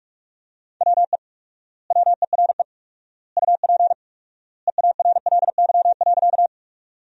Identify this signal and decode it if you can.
Morse code, heard as a keyed tone, about 46 wpm, 700 Hz. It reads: WE WERE UP EARLY.
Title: 4x Word-Spacing - Sentences from Top 400 Words - 45wpm